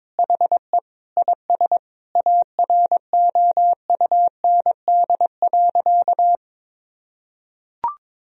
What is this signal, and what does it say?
Morse code, heard as a keyed tone, about 22 wpm, 700 Hz. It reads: HE IS AROUND.
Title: Sentences from Top 1000 Words - Review Part 4 - 22wpm